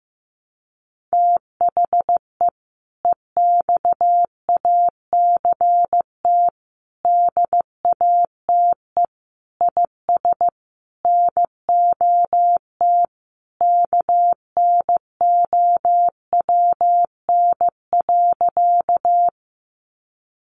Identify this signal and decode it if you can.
Morse code, heard as a keyed tone, about 15 wpm, 700 Hz. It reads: THE EXACT DATE IS NOT KNOWN.
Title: Sentences from Top 1300 Words 15wpm